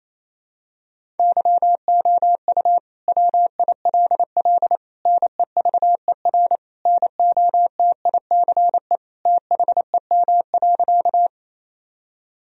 Morse code, heard as a keyed tone, about 28 wpm, 700 Hz. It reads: YOU WILL NEVER NOTICE THEM.
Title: Sentences from Top 1000 Words - Review Part 2 - 28wpm